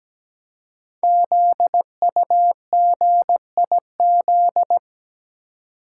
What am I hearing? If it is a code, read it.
Morse code, heard as a keyed tone, about 17 wpm, 700 Hz. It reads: ZUGIZ
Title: Code Groups - 5 Letter:Numbers - 17wpm